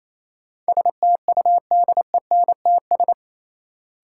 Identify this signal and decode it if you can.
Morse code, heard as a keyed tone, about 28 wpm, 700 Hz. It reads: STUDENTS
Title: Sets of 5 Words Encore - New Word Drill - 28wpm